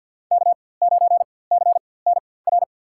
Morse code, heard as a keyed tone, about 50 wpm, spaced as for 23 wpm, 690 Hz. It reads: K9CNR
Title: USA Call Signs - 5 Characters - 4x Character Spacing - 50wpm